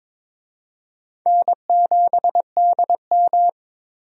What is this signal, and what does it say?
Morse code, heard as a keyed tone, about 23 wpm, 700 Hz. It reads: N7DM
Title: USA Call Signs - 4 Characters - 22wpm